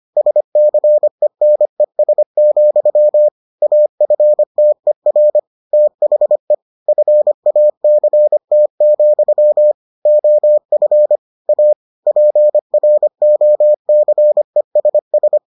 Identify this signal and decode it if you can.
Morse code, heard as a keyed tone, about 25 wpm, 600 Hz. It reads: SCENES, AFTER THE FACT, OF A PROCESS